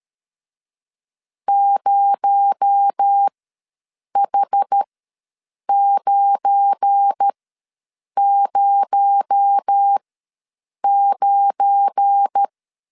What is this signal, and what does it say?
Morse code, heard as a keyed tone, about 13 wpm, spaced as for 7 wpm, 800 Hz. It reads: 0H909
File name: LEZIONI / 7 GF9Z / 7 GF9 ws 12wpm 3344